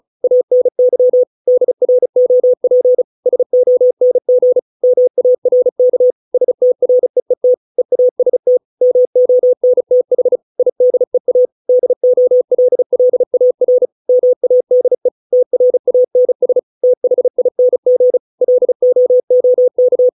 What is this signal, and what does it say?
Morse code, heard as a keyed tone, about 34 wpm, 500 Hz. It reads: ANY DROP SONG MARK STREET EAST MONTH IDEA DOLLAR MADE TRANS THING LOOK
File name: 632 words / 632 Words 35WPM